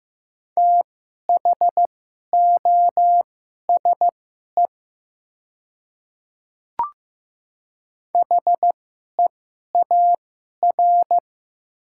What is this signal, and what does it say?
Morse code, heard as a keyed tone, about 15 wpm, spaced as for 11 wpm, 700 Hz. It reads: THOSE HEAR
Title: 2x Character Spacing - Top 300 Words - 15wpm